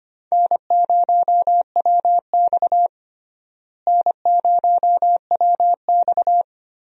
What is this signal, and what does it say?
Morse code, heard as a keyed tone, about 25 wpm, 700 Hz. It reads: N0WX N0WX
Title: USA Call Signs - 4 Characters - Repeated 3x - 25wpm